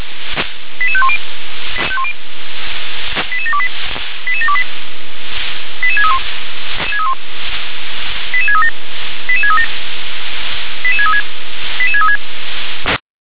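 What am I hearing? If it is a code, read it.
Five-tone selective call (ZVEI), heard as quick runs of five tones, tones of about 70 ms. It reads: 88510 88728 88728 88631 88637 88637 88636 88636
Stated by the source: ZVEI five-tone selective call received on an SDR.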